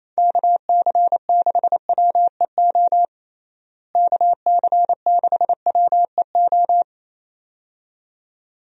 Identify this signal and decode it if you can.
Morse code, heard as a keyed tone, about 28 wpm, 700 Hz. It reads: KC6WEO KC6WEO
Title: USA Call Signs - 6 Characters - Repeated 2x - 28wpm